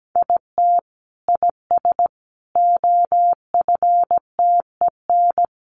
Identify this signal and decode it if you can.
Morse code, heard as a keyed tone, about 17 wpm, 700 Hz. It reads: IT IS OFTEN
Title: Sentences from Top 1400 Words 17wpm